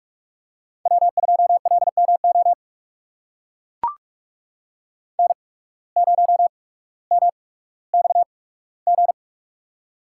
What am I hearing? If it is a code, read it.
Morse code, heard as a keyed tone, about 43 wpm, spaced as for 19 wpm, 700 Hz. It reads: W2LMO N0MXG
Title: USA Call Signs - 5 Characters - 8x Character Spacing - 45wpm